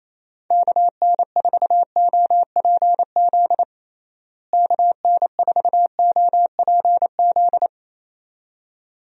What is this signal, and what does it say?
Morse code, heard as a keyed tone, about 28 wpm, 700 Hz. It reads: KN4OPZ KN4OPZ
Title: USA Call Signs - 6 Characters - Repeated 2x - 28wpm